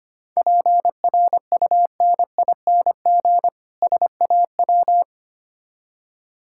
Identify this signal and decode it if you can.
Morse code, heard as a keyed tone, about 25 wpm, 700 Hz. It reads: PRUNING SAW
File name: practice / 25wpm / Difficult Words - Speed-Racing with Hints - 25wpm